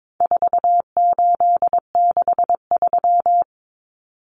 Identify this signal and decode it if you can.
Morse code, heard as a keyed tone, about 22 wpm, 700 Hz. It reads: 4863
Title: Common Cut Numbers - 4-Digit 22wpm